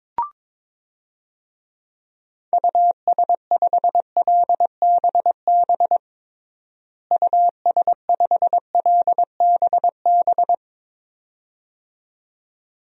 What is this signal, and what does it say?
Morse code, heard as a keyed tone, about 22 wpm, 700 Hz. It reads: US5LBB US5LBB
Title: DXCC-100 Call Signs - 6 Characters - Repeated 2x - 22wpm